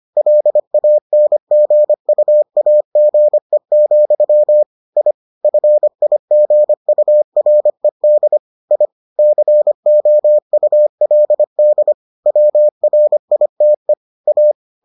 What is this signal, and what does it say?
Morse code, heard as a keyed tone, about 25 wpm, 600 Hz. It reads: LANGUAGE, I FIGURED I COULD WRITE A